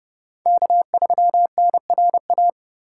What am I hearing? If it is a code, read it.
Morse code, heard as a keyed tone, about 37 wpm, 700 Hz. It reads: K3NRA
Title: Speed Racing - USA 1x3 Call Signs 30wpm